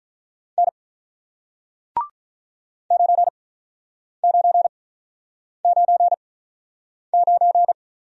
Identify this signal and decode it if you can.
Morse code, heard as a keyed tone, about 44 wpm, spaced as for 9 wpm, 700 Hz. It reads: N 9999